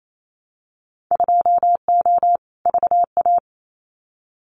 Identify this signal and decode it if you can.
Morse code, heard as a keyed tone, about 28 wpm, 700 Hz. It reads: 2O VA
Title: Winter Field Day Exchanges 28wpm